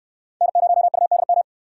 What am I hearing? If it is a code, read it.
Morse code, heard as a keyed tone, about 69 wpm, 700 Hz. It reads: N0UDM